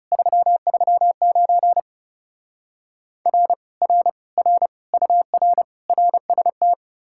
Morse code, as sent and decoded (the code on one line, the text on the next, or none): ...-- ...-- ----. / .-. / .-. / .-. / ..- .-. / .-. ... -
339 R R R UR RST